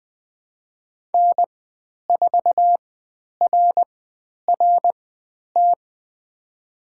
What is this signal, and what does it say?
Morse code, heard as a keyed tone, about 20 wpm, spaced as for 10 wpm, 700 Hz. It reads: N4RRT
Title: 095 Lesson 16g - Letter c - Call Signs 20wpm